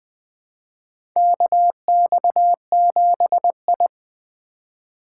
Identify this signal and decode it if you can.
Morse code, heard as a keyed tone, about 21 wpm, 700 Hz. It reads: KX7I
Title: USA Call Signs - 4 Characters - 3x Character Spacing - 20wpm